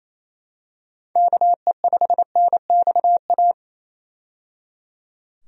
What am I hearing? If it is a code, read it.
Morse code, heard as a keyed tone, about 27 wpm, 700 Hz. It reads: KE5NXA
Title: USA Call Signs - 6 Characters - 28wpm